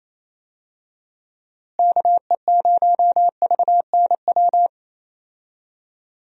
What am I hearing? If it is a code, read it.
Morse code, heard as a keyed tone, about 28 wpm, 700 Hz. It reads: KE0VNW
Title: USA Call Signs - 6 Characters - Repeated 3x - 28wpm